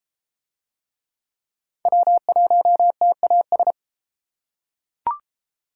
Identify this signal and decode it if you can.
Morse code, heard as a keyed tone, about 33 wpm, 700 Hz. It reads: W1TAS